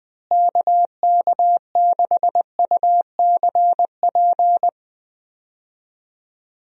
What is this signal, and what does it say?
Morse code, heard as a keyed tone, about 20 wpm, 700 Hz. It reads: KK6UCP
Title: USA Call Signs - 6 Characters - Repeated 2x - 20wpm